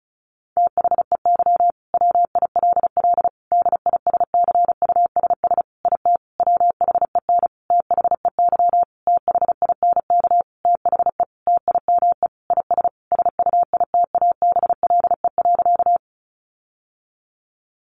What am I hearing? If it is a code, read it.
Morse code, heard as a keyed tone, about 34 wpm, 700 Hz. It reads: THEY WILL DISCUSS IT WHEN THEY THINK THE TIME IS SUITABLE.
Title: Sentences from Top 1400 Words 35wpm